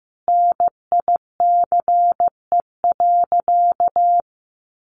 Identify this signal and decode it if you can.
Morse code, heard as a keyed tone, about 15 wpm, 700 Hz. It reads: NICE.
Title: Additional Easy Phrases 15wpm